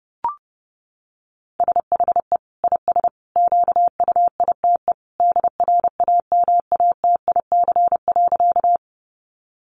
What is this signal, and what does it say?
Morse code, heard as a keyed tone, about 30 wpm, 700 Hz. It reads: SHE IS QUITE DRAMATIC.
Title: Sentences from Top 1300 Words 30wpm